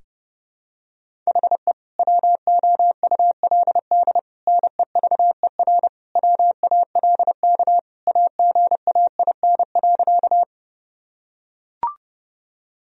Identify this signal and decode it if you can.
Morse code, heard as a keyed tone, about 30 wpm, 700 Hz. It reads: HE WOULD NEVER WALK AGAIN.
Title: Sentences from Top 1000 Words - Review Part 1 - 30wpm